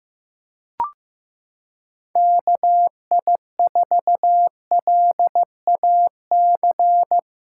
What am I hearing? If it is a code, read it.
Morse code, heard as a keyed tone, about 15 wpm, 700 Hz. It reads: KI4LAC
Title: USA Callsign DE Callsign - 6 Characters - 15wpm